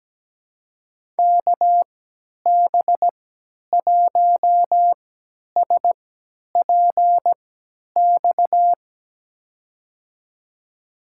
Morse code, heard as a keyed tone, about 17 wpm, spaced as for 10 wpm, 700 Hz. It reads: KB1SPX
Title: USA Call Signs - 6 Characters - 3x Character Spacing - 17wpm